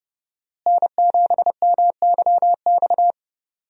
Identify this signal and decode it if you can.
Morse code, heard as a keyed tone, about 31 wpm, 700 Hz. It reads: N7MYX
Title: USA Call Signs - 5 Characters - Repeated 2x - 30wpm